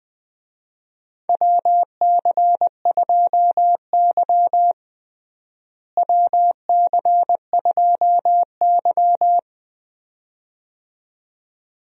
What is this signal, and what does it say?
Morse code, heard as a keyed tone, about 20 wpm, 700 Hz. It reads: WC2Y WC2Y
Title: USA Call Signs - 4 Characters - Repeated 2x - 20wpm